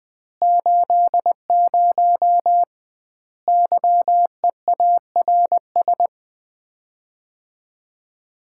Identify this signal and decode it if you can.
Morse code, heard as a keyed tone, about 20 wpm, 700 Hz. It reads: 80 YEARS
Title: Common QSO Phrases - 2x Word-Spacing - 20wpm